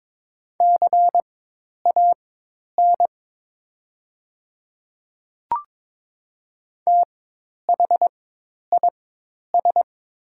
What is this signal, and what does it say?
Morse code, heard as a keyed tone, about 22 wpm, spaced as for 10 wpm, 700 Hz. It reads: CAN THIS
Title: 4x Character Spacing - Top 100 Words - 22wpm